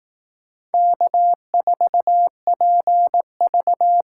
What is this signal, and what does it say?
Morse code, heard as a keyed tone, about 18 wpm, 700 Hz. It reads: K4PV
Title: Speed Racing - USA 1x2 Call Signs 15wpm